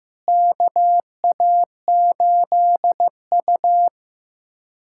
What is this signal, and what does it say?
Morse code, heard as a keyed tone, about 16 wpm, 700 Hz. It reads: KA8U
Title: USA Callsign DE Callsign - 4 Characters - 15wpm